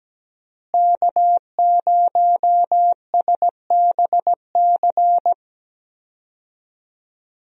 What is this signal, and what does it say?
Morse code, heard as a keyed tone, about 17 wpm, 700 Hz. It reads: K0SBC
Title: USA Call Signs - 5 Characters - Repeated 3x - 17wpm